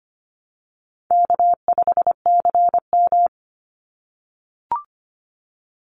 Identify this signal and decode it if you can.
Morse code, heard as a keyed tone, about 25 wpm, 700 Hz. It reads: K5CM
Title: Easy Call Signs - 25wpm